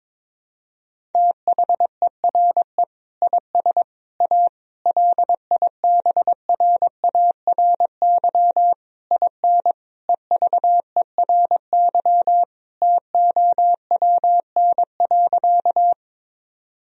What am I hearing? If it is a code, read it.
Morse code, heard as a keyed tone, about 22 wpm, 700 Hz. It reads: THERE IS A LIBRARY IN EVERY TOWN.
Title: Sentences from Top 1000 Words - Review Part 4 - 22wpm